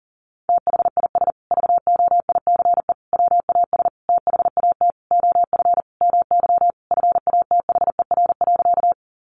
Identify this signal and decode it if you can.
Morse code, heard as a keyed tone, about 40 wpm, 700 Hz. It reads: THIS VOICE WAS THAT OF MY FATHER.
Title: Sentences from Top 500 Words 40wpm